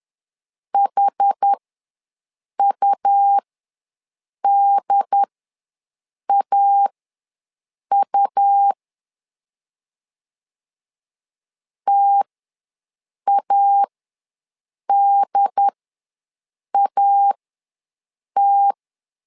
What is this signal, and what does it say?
Morse code, heard as a keyed tone, about 10 wpm, spaced as for 6 wpm, 800 Hz. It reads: HUDAU TADAT
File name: LEZIONI / 2 UHDN / 2 UHD ws 10wpm 2058